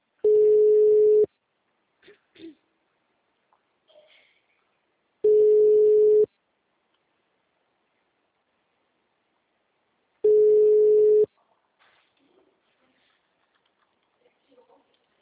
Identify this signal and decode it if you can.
Ringback tone (Europe, 425 Hz), with speech.